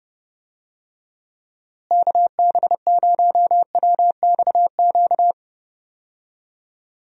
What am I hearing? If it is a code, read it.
Morse code, heard as a keyed tone, about 29 wpm, 700 Hz. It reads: KB0WXQ
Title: USA Call Signs - 6 Characters - Repeated 2x - 30wpm